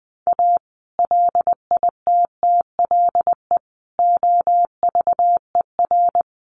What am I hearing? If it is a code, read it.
Morse code, heard as a keyed tone, about 20 wpm, 700 Hz. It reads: A LITTLE OVER
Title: Sentences from Top 1300 Words 20wpm